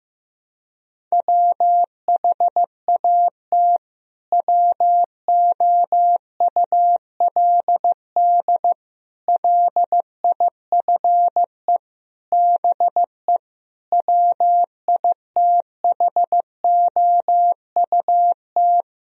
Morse code, heard as a keyed tone, about 15 wpm, 700 Hz. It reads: WHAT WOULD LIFE BE WITHOUT
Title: Sentences from Top 1000 Words - Review Part 3 - 15wpm